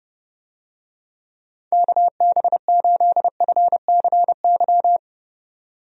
Morse code, heard as a keyed tone, about 32 wpm, 700 Hz. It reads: KB8FCY
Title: USA Call Signs - 6 Characters - 30wpm